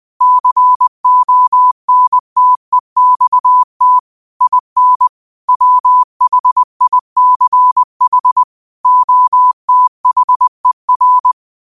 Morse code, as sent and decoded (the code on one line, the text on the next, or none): -.-. --- -. - . -..- - / .. -. / .-- .... .. -.-. .... / --- - .... .-.
CONTEXT IN WHICH OTHER